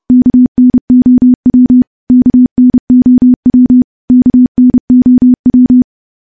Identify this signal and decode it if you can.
Morse code, heard as a keyed tone, about 30 wpm, 250 Hz. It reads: KNOW KNOW KNOW
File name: morse / morse-training-30WPM